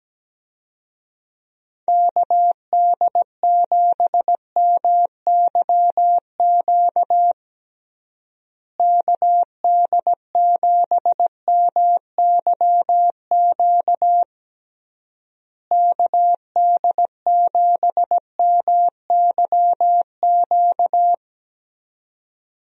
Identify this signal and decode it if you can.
Morse code, heard as a keyed tone, about 17 wpm, 700 Hz. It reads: KD7MYQ KD7MYQ KD7MYQ
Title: USA Call Signs - 6 Characters - Repeated 3x - 17wpm